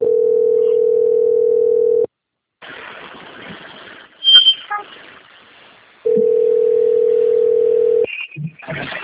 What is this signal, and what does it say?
Ringback tone (North America), 440 + 480 Hz.